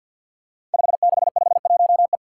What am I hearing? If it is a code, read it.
Morse code, heard as a keyed tone, about 52 wpm, 700 Hz. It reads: 5651E
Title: Five Challenging Letter-Number Combinations 50wpm